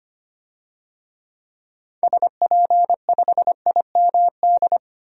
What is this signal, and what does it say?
Morse code, heard as a keyed tone, about 25 wpm, 700 Hz. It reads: SP5IMD
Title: DXCC-100 Call Signs - 6 Characters - 25wpm